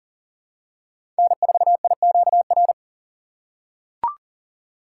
Morse code, heard as a keyed tone, about 40 wpm, 700 Hz. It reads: N4IQR
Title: USA Call Signs - 6x Character Spacing - 40wpm